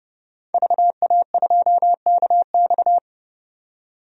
Morse code, heard as a keyed tone, about 32 wpm, 700 Hz. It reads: VA2KX
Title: DXCC-100 Call Signs - 5 Characters - 30wpm